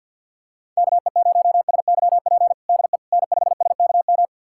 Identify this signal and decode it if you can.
Morse code, heard as a keyed tone, about 46 wpm, 700 Hz. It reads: KE0SYP DE N5SKM
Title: USA Callsign DE Callsign 50wpm